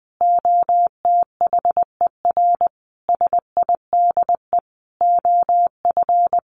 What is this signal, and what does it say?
Morse code, heard as a keyed tone, about 20 wpm, 700 Hz. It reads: OTHER SIDE OF
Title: Sentences from Top 1400 Words 20wpm